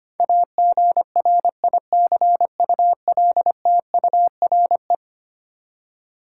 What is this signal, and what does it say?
Morse code, heard as a keyed tone, about 25 wpm, 700 Hz. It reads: AGRICULTURE
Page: morse-code-ninja-practice-25wpm